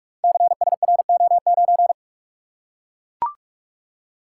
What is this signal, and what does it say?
Morse code, heard as a keyed tone, about 44 wpm, 700 Hz. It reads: CSRO9
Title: Code Groups - 5 Letter:Numbers - 45wpm